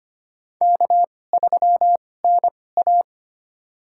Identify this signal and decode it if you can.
Morse code, heard as a keyed tone, about 25 wpm, spaced as for 18 wpm, 700 Hz. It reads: K3NA
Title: USA Call Signs - 4 Characters - 2x Character Spacing - 25wpm